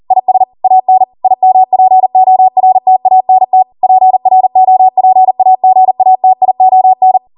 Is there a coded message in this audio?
Yes: Morse code from 0.1 to 7.2 s, heard as a keyed tone, about 40 wpm, 750 Hz. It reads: IS AN IMPORTANT PROPAGATION